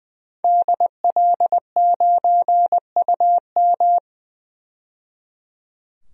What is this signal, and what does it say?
Morse code, heard as a keyed tone, about 20 wpm, 700 Hz. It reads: DL9UM
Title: DXCC-100 Call Signs - 5 Characters - 20wpm